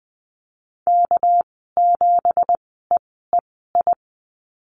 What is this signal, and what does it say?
Morse code, heard as a keyed tone, about 20 wpm, spaced as for 15 wpm, 700 Hz. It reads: K7EEI
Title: 200 Lesson 32g - Letter k - Call Signs 20wpm